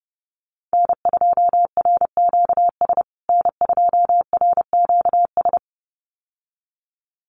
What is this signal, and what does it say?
Morse code, heard as a keyed tone, about 28 wpm, 700 Hz. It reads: N2RQS N2RQS
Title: USA Call Signs Repeated 2x 30wpm